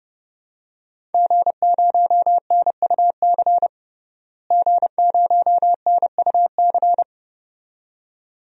Morse code, heard as a keyed tone, about 30 wpm, 700 Hz. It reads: G0NUC G0NUC
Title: DXCC-100 Call Signs - 5 Characters - Repeated 2x - 30wpm